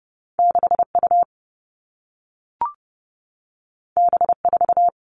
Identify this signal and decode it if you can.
Morse code, heard as a keyed tone, about 30 wpm, 700 Hz. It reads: 6 U B 4
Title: All Cut Numbers - 2-Digit 30wpm